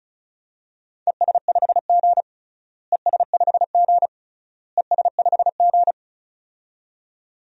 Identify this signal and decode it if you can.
Morse code, heard as a keyed tone, about 34 wpm, 700 Hz. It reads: ES5G ES5G ES5G
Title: DXCC-100 Call Signs - 4 Characters - Repeated 3x - 35wpm